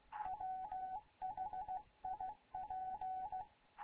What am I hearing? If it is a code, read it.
Morse code, heard as a keyed tone, about 16 wpm, 750 Hz. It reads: WHIP
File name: Morze-HA2MN / hangp / 2 11 GYAKORLAT